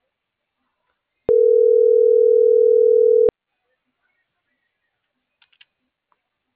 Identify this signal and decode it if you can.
Ringback tone (North America), 440 + 480 Hz.